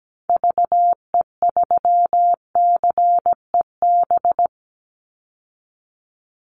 Morse code, heard as a keyed tone, about 17 wpm, 700 Hz. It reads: VE3CEB
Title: Call Signs - 17wpm